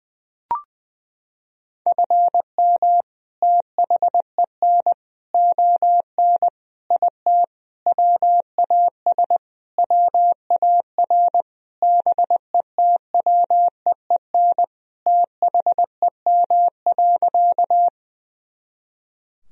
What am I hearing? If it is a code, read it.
Morse code, heard as a keyed tone, about 20 wpm, 700 Hz. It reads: FM THEN ON IT WAS WAR BETWEEN THEM.
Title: Sentences from Top 400 Words and Abbreviations 20wpm